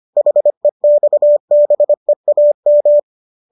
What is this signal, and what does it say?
Morse code, heard as a keyed tone, about 25 wpm, 590 Hz. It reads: HEXBEAM